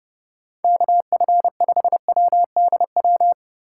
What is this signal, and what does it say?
Morse code, heard as a keyed tone, about 29 wpm, 700 Hz. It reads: KF5WDW